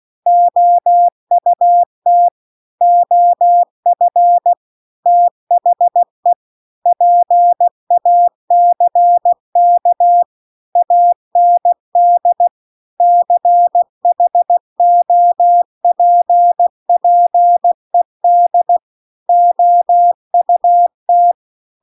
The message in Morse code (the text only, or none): OUT OF THE PACK AND CHOPPED OUT